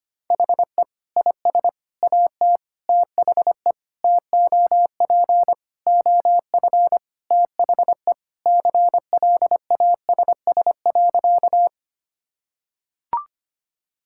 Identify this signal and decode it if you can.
Morse code, heard as a keyed tone, about 25 wpm, 700 Hz. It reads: HE IS AT THE TOP OF THE CLASS.
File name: practice / 25wpm / Sentences with 3-Word Phrases - 25wpm